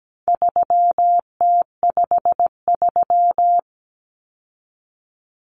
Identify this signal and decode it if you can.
Morse code, heard as a keyed tone, about 17 wpm, 700 Hz. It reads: 3T53
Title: Common Cut Numbers - 4-Digit 17wpm